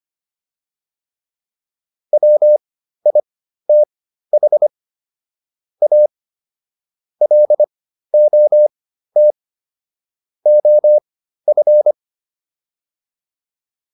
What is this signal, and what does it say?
Morse code, heard as a keyed tone, about 25 wpm, spaced as for 13 wpm, 600 Hz. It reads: WITH A LOT OF